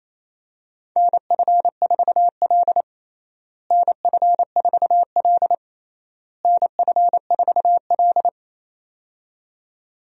Morse code, heard as a keyed tone, about 28 wpm, 700 Hz. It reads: NF4L NF4L NF4L
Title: USA Call Signs - 4 Characters - Repeated 3x - 28wpm